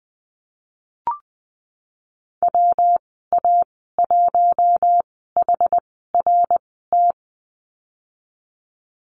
Morse code, heard as a keyed tone, about 21 wpm, spaced as for 15 wpm, 700 Hz. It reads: WA1HRT
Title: 122 Lesson 20g - Number 3 - Call Signs 20wpm